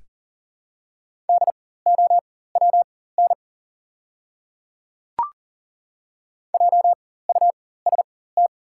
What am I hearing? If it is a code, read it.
Morse code, heard as a keyed tone, about 40 wpm, spaced as for 19 wpm, 700 Hz. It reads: DOWN JUST
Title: 4x Character Spacing - Top 200 Words - 40wpm